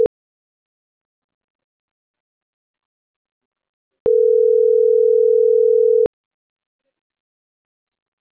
Ringback tone (North America), 440 + 480 Hz.